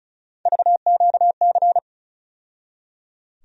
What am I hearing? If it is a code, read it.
Morse code, heard as a keyed tone, about 36 wpm, 700 Hz. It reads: VQC
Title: USA Call Sign Postfixes - 3 Letters 35wpm